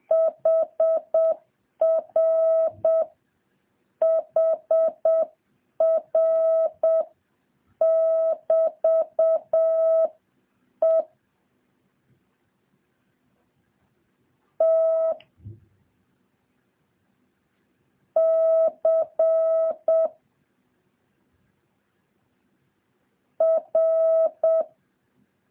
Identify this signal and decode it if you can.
Morse code, heard as a keyed tone, about 7 wpm, 650 Hz. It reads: HR HR <BT> E T C R
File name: jh8cbh / 2023cw